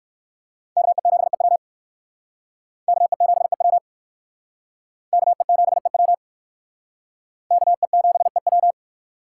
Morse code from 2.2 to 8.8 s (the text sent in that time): KE7EW KE7EW KE7EW